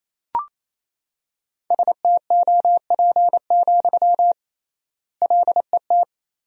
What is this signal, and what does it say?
Morse code, heard as a keyed tone, about 28 wpm, 700 Hz. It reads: STOP, LET
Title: 3x Word-Spacing - Sentences from Top 300 Words - 28wpm